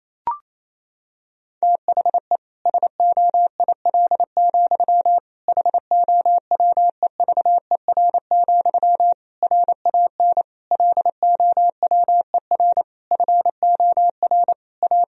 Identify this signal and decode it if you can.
Morse code, heard as a keyed tone, about 28 wpm, 700 Hz. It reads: THE SOIL, HOWEVER, RAN LOWER FOR A
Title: Sentences from Top 1000 Words - Review Part 4 - 28wpm